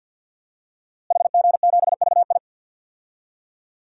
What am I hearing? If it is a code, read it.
Morse code, heard as a keyed tone, about 51 wpm, 700 Hz. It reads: HG7VI
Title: DXCC-100 Call Signs - 5 Characters - Repeated 3x - 50wpm